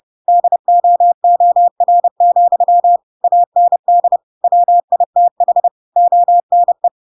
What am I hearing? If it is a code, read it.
Morse code, heard as a keyed tone, about 30 wpm, 700 Hz. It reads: DOOR, AND WITH ONE
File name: Pinocchio / Pinocchio ch3 30 30 ik1hge